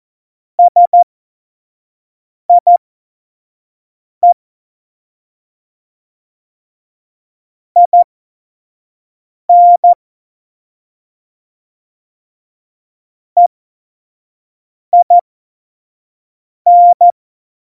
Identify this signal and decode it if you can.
Morse code, heard as a keyed tone, about 14 wpm, spaced as for 5 wpm, 700 Hz. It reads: SIE IN EIN
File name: Morsekurs / Witze / Witz 25 Juli 17, Tempo 25 F70